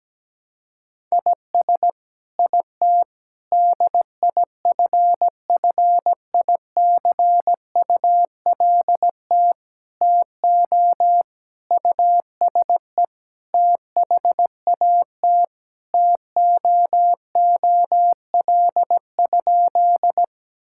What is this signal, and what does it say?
Morse code, heard as a keyed tone, about 17 wpm, 700 Hz. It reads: IS IT DIFFICULT TO USE THAT TOOL?
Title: Sentences from Top 1000 Words - Review Part 6 - 17wpm